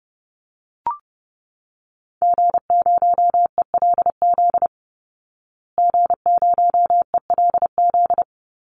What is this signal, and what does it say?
Morse code, heard as a keyed tone, about 30 wpm, 700 Hz. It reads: G0ELZ G0ELZ
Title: Call Signs Repeated 3x - 30wpm